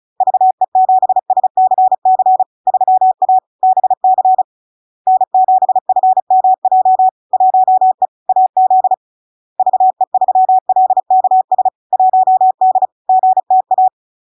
Morse code, heard as a keyed tone, about 36 wpm, 750 Hz. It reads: VE7SCC 3A BC N7FMJ 1E AZ VE3LKS 1D GTA
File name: LARK-FD-2022-Log-in-CW-35wpm-750-Hz